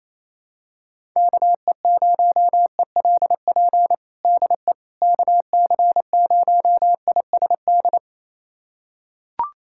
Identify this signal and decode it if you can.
Morse code, heard as a keyed tone, about 28 wpm, 700 Hz. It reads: KE0ELP DE KC0ISD
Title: USA Callsign DE Callsign - 6 Characters - 28wpm